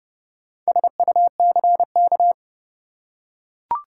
Morse code, heard as a keyed tone, about 30 wpm, 700 Hz. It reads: SUCK
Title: Four Letter Words - 30wpm